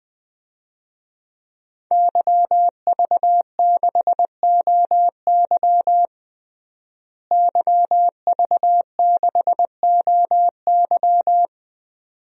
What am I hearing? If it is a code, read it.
Morse code, heard as a keyed tone, about 20 wpm, 700 Hz. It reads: YV6OY YV6OY
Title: DXCC-100 Call Signs - 5 Characters - Repeated 2x - 20wpm